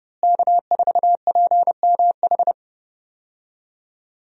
Morse code, heard as a keyed tone, about 30 wpm, 700 Hz. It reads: K4PMH
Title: USA Call Signs - 5 Characters - Repeated 3x - 30wpm